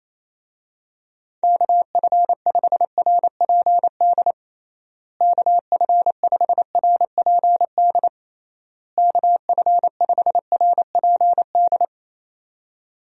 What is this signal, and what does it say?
Morse code, heard as a keyed tone, about 28 wpm, 700 Hz. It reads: KF5RPD KF5RPD KF5RPD
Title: USA Call Signs - 6 Characters - Repeated 3x - 28wpm